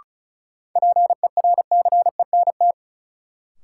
Morse code, heard as a keyed tone, about 36 wpm, 700 Hz. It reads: PERCENT
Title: Sets of 2 Words - New Word Drill - 35wpm